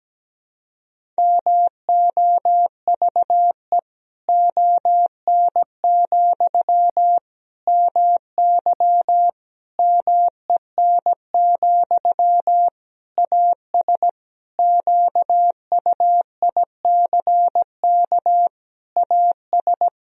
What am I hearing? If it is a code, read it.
Morse code, heard as a keyed tone, about 17 wpm, 700 Hz. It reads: MOVE ON, MY MEN, AS QUICK AS
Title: Sentences from Top 500 Words and Abbreviations 17wpm